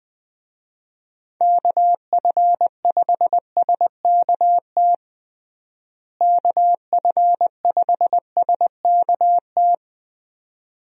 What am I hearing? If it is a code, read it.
Morse code, heard as a keyed tone, about 20 wpm, 700 Hz. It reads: KF5SKT KF5SKT
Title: USA Call Signs - 6 Characters - Repeated 2x - 20wpm